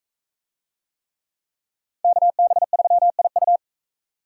Morse code, heard as a keyed tone, about 41 wpm, 700 Hz. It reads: KB3IU